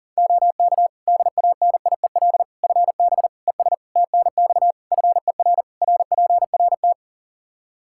Morse code, heard as a keyed tone, about 40 wpm, 700 Hz. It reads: OK DANIEL FB ES TNX FER RPRT
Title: K1USN SST QSO Practice 40wpm